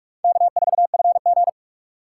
Morse code, heard as a keyed tone, about 46 wpm, 700 Hz. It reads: K4FG
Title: USA Call Signs - 4 Characters - 2x Character Spacing - 45wpm